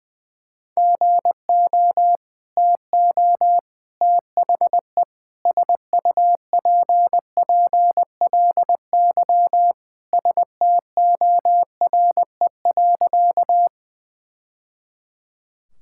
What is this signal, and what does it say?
Morse code, heard as a keyed tone, about 21 wpm, 700 Hz. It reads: GO TO THE SUPPLY STORE.
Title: Sentences from Top 1000 Words - Review Part 5 - 20wpm